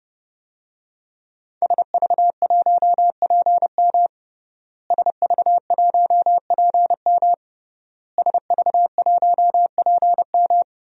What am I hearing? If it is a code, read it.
Morse code, heard as a keyed tone, about 30 wpm, 700 Hz. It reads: SV1PM SV1PM SV1PM
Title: DXCC-100 Call Signs - 5 Characters - Repeated 3x - 30wpm